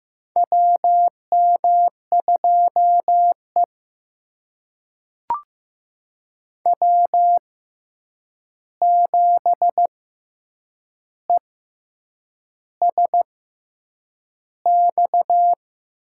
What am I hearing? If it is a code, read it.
Morse code, heard as a keyed tone, about 15 wpm, spaced as for 7 wpm, 700 Hz. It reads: WM2E W7ESX
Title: USA Call Signs - 6x Character Spacing - 15wpm